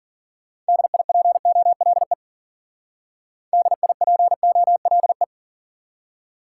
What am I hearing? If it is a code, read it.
Morse code, heard as a keyed tone, about 50 wpm, 700 Hz. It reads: DIPOLE DIPOLE
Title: Speed-Racing Top 90 QSO Elements 40wpm